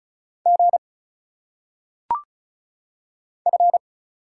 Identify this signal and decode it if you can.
Morse code, heard as a keyed tone, about 34 wpm, 700 Hz. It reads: G F
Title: Single Letters - Phonetic Pronunciation 35wpm